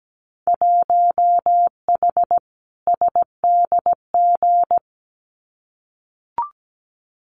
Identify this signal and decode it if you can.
Morse code, heard as a keyed tone, about 17 wpm, 700 Hz. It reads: 1H SDG
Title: Winter Field Day Exchanges 17wpm